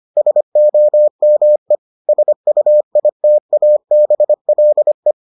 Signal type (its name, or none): Morse code